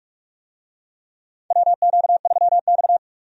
Morse code, heard as a keyed tone, about 46 wpm, 700 Hz. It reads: WQ3X
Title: USA Call Signs - 4 Characters - 45wpm